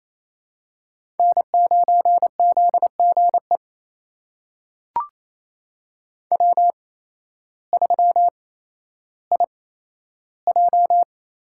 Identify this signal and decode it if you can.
Morse code, heard as a keyed tone, about 28 wpm, spaced as for 13 wpm, 700 Hz. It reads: N9ZGE W3IJ
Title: USA Call Signs - 5 Characters - 8x Character Spacing - 28wpm